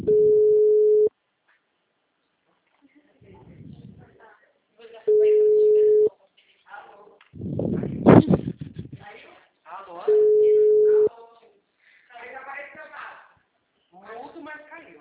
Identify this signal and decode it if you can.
Ringback tone (Europe, 425 Hz), with speech.
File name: out- -2013- - - .32251